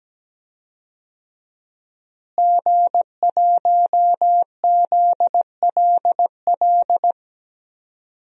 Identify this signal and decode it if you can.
Morse code, heard as a keyed tone, about 17 wpm, 700 Hz. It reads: G1ZLL
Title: DXCC-100 Call Signs - 5 Characters - Repeated 3x - 17wpm